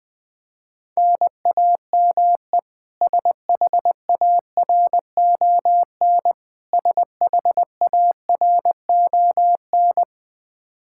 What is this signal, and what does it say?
Morse code, heard as a keyed tone, about 20 wpm, 700 Hz. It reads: NAME SHARON SHARON
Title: K1USN SST QSO Practice 20wpm